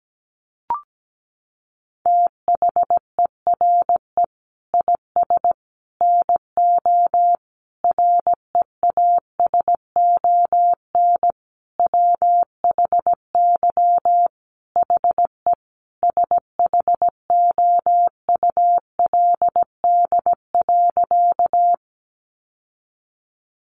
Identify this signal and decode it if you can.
Morse code, heard as a keyed tone, about 17 wpm, 700 Hz. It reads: THERE IS NO REASON WHY HE SHOULD.
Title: Sentences from Top 700 Words 17wpm